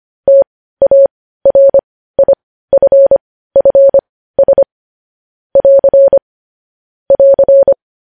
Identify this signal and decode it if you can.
Morse code, heard as a keyed tone, about 25 wpm, spaced as for 15 wpm, 550 Hz. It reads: TARIFFS <AR> <AR>
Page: news headlines in morse code at 15 wpm